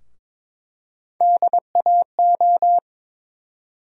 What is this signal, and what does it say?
Morse code, heard as a keyed tone, about 22 wpm, 700 Hz. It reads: DAO
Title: USA Call Sign Postfixes - 3 Letters 22wpm